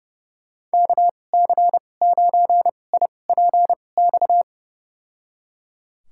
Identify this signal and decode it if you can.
Morse code, heard as a keyed tone, about 30 wpm, spaced as for 22 wpm, 700 Hz. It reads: KC9IPX
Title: USA Call Signs - 6 Characters - 2x Character Spacing - 30wpm